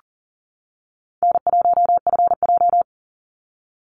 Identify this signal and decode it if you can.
Morse code, heard as a keyed tone, about 40 wpm, 700 Hz. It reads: N1FJ
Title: NA SOTA Activator Callsigns 40wpm